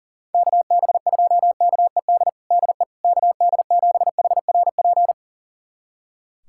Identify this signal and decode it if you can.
Morse code, heard as a keyed tone, about 40 wpm, 700 Hz. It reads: KB2KED DE KD7HRP
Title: USA Callsign DE Callsign 40wpm